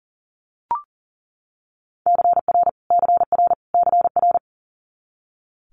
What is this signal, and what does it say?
Morse code, heard as a keyed tone, about 40 wpm, 700 Hz. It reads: CR CR CR
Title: Multi-letter Phonemes Repeated 3x 40wpm